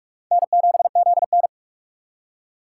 Morse code, heard as a keyed tone, about 42 wpm, 700 Hz. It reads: N7ZN
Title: K1USN SST QSO Practice - 45wpm